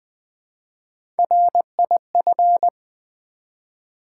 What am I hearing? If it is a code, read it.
Morse code, heard as a keyed tone, about 20 wpm, 700 Hz. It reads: RIF
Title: Build A Word - 1 Letter At A Time - 3 to 5-Letter Words - 20wpm